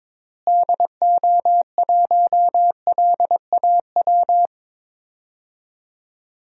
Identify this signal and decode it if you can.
Morse code, heard as a keyed tone, about 22 wpm, 700 Hz. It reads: DO1LAW
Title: DXCC-100 Call Signs - 6 Characters - Repeated 2x - 22wpm